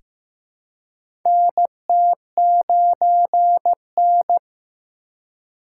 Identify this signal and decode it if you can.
Morse code, heard as a keyed tone, about 15 wpm, 700 Hz. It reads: NT9N
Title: USA Call Signs - 3x Character Spacing - 15wpm